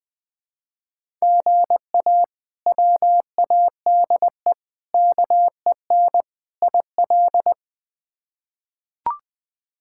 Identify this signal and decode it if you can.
Morse code, heard as a keyed tone, about 20 wpm, 700 Hz. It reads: GA WADE KEN IL
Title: K1USN SST QSO Practice - 20wpm